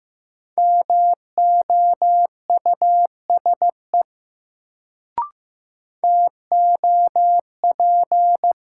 Morse code, heard as a keyed tone, about 15 wpm, 700 Hz. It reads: MOUSE TOP
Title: Sentences from Top 700 Words - New Word Drill - 15wpm